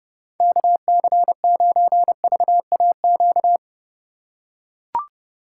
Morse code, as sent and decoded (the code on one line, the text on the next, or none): -.- -.-. ----. ...- .- --.-
KC9VAQ